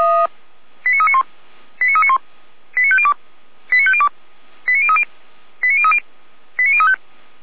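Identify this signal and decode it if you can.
Five-tone selective call (ZVEI), heard as quick runs of five tones, tones of about 70 ms. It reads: 89391 89391 89502 89502 89030 89030 89035
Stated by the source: ZVEI five-tone call received by SDR.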